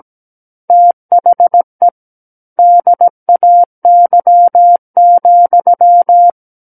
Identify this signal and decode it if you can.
Morse code, heard as a keyed tone, about 17 wpm, 700 Hz. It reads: THE DAY,
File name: Book One / 08-Chapter 8 - 17 WPM